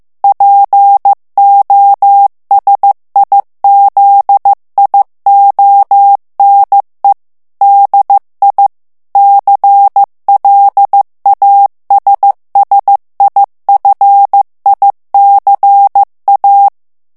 Morse code, heard as a keyed tone, about 15 wpm, 800 Hz. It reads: POSIZIONE DI CLASSIFICA
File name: morse mp3 / ITALIEN texte clair 2 vitesse 14 mots minute